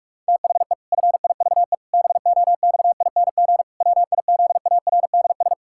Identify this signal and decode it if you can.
Morse code, heard as a keyed tone, about 44 wpm, 700 Hz. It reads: THE FIVE BOXING WIZARDS
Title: Pangram Challenge v2 - 45wpm